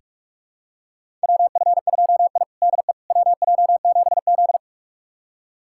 Morse code, heard as a keyed tone, about 41 wpm, 700 Hz. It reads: WF2I DE WJ7Z
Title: USA Callsign DE Callsign - 4 Characters - 45wpm